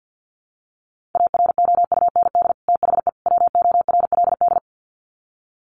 Morse code, heard as a keyed tone, about 50 wpm, 700 Hz. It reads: AROUND THE WORLD
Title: Sets of 3 Words 50wpm